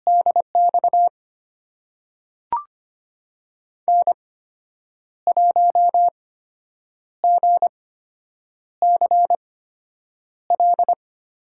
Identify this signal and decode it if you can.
Morse code, heard as a keyed tone, about 25 wpm, spaced as for 8 wpm, 700 Hz. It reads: DX N1GCL
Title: USA Call Signs - 5 Characters - 8x Character Spacing - 25wpm